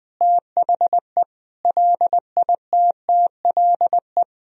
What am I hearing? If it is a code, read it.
Morse code, heard as a keyed tone, about 20 wpm, 700 Hz. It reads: THE LITTLE
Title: Sentences from Top 200 Words and Abbreviations 20wpm